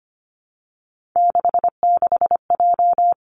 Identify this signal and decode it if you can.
Morse code, heard as a keyed tone, about 25 wpm, 700 Hz. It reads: 66J
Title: Three Challenging Letter-Number Combinations 25wpm